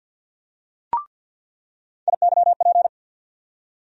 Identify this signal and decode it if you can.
Morse code, heard as a keyed tone, about 46 wpm, 700 Hz. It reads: IYP
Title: USA Call Sign Postfixes - 3 Letters 50wpm